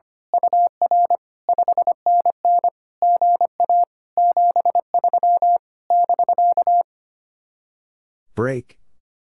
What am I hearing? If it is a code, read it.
Morse code, heard as a keyed tone, about 24 wpm, 700 Hz. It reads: UR 5NN GA 73 <BK>
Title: Speed-Racing Common QSO Phrases - 25wpm